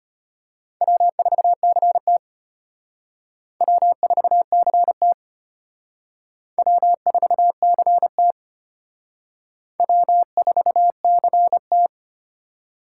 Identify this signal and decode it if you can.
Morse code, heard as a keyed tone, about 34 wpm, 700 Hz. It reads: W4CT W4CT W4CT W4CT